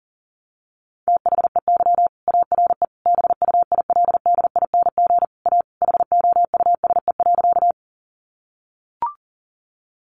Morse code, heard as a keyed tone, about 40 wpm, 700 Hz. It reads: THEY ARE BUILDING A HOUSE.
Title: Sentences from Top 700 Words 40wpm